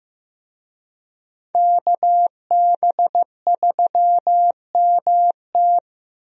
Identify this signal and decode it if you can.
Morse code, heard as a keyed tone, about 15 wpm, 700 Hz. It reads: KB3MT
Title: USA Call Signs - 5 Characters - 15wpm